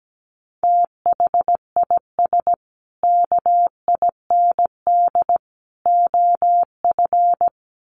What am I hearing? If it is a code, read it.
Morse code, heard as a keyed tone, about 17 wpm, 700 Hz. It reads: THIS KIND OF
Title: Sets of 3 Words 17wpm